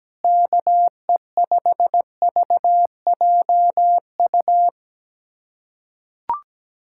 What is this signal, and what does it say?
Morse code, heard as a keyed tone, about 17 wpm, 700 Hz. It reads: KE5VJU
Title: USA Callsign DE Callsign 17wpm